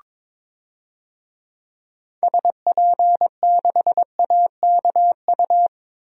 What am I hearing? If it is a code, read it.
Morse code, heard as a keyed tone, about 22 wpm, 700 Hz. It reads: SP6AKU
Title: DXCC-100 Call Signs - 6 Characters - 22wpm